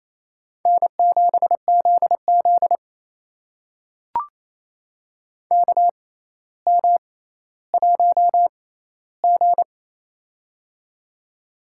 Morse code, heard as a keyed tone, about 28 wpm, spaced as for 14 wpm, 700 Hz. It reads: N7ZZ KM1G
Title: USA Call Signs - 4 Characters - 6x Character Spacing - 28wpm